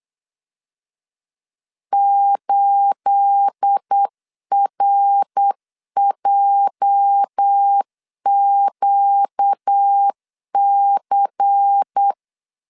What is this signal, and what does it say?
Morse code, heard as a keyed tone, about 8 wpm, 800 Hz. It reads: 8RJQC
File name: LEZIONI / 9 28 8wpm 3512